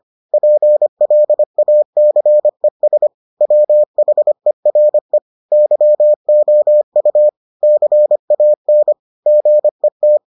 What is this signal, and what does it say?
Morse code, heard as a keyed tone, about 25 wpm, 600 Hz. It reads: PLACES WHERE YOU CAN GET